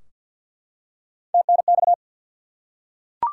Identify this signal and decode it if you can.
Morse code, heard as a keyed tone, about 52 wpm, 700 Hz. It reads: TNX
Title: Top 90 QSO Elements 50wpm